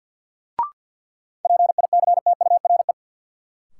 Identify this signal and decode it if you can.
Morse code, heard as a keyed tone, about 52 wpm, 700 Hz. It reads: PICTURE
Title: Sentences from Top 500 Words - New Word Drill - 50wpm